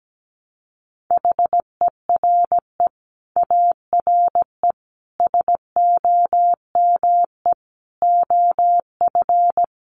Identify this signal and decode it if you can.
Morse code, heard as a keyed tone, about 17 wpm, 700 Hz. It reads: HERE ARE SOME OF
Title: Sets of 5 Words Encore 17wpm